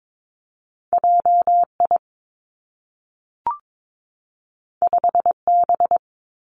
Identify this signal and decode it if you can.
Morse code, heard as a keyed tone, about 22 wpm, 700 Hz. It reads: J I 5 B
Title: Two Challenging Letter-Number Combinations 22wpm